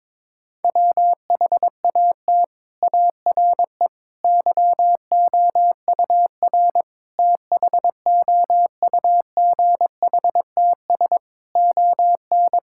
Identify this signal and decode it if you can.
Morse code, heard as a keyed tone, about 22 wpm, 700 Hz. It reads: WHAT ARE YOUR THOUGHTS ON
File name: practice / 22wpm / Sets of 6 Words - 22wpm